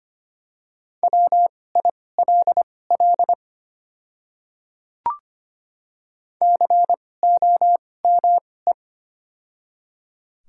Morse code, heard as a keyed tone, about 25 wpm, spaced as for 18 wpm, 700 Hz. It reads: WILL COME
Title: 2x Character Spacing - Top 100 Words - 25wpm